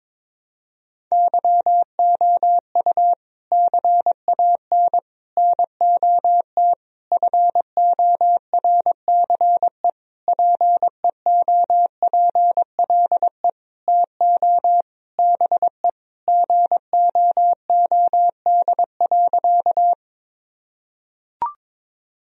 Morse code, heard as a keyed tone, about 22 wpm, 700 Hz. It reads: YOU CAN NOT FORCE PEOPLE TO BE GOOD.
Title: Sentences from Top 1000 Words - Review Part 3 - 22wpm